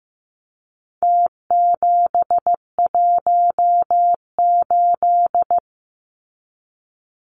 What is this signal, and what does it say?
Morse code, heard as a keyed tone, about 15 wpm, 700 Hz. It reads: T718
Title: Numbers - Contest Style - 4-Digit 15wpm